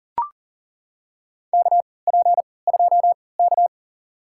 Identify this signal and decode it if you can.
Morse code, heard as a keyed tone, about 38 wpm, spaced as for 22 wpm, 700 Hz. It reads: KP2K
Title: USA Call Signs - 4 Characters - 3x Character Spacing - 40wpm